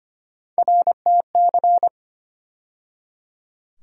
Morse code, heard as a keyed tone, about 25 wpm, 700 Hz. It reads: RTC
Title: USA Call Sign Postfixes - 3 Letters 25wpm